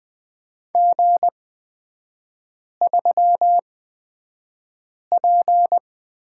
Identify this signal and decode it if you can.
Morse code, heard as a keyed tone, about 20 wpm, spaced as for 5 wpm, 700 Hz. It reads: G3P
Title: 243 Lesson 39e - Letter z - Three Characters 20wpm